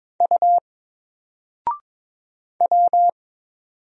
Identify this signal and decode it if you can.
Morse code, heard as a keyed tone, about 22 wpm, 700 Hz. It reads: U W
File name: Single-Letter-Number-22wpm